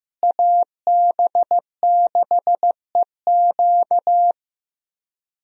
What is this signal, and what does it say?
Morse code, heard as a keyed tone, about 15 wpm, 700 Hz. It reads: AB6EQ
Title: USA Call Signs - 5 Characters - Repeated 3x - 15wpm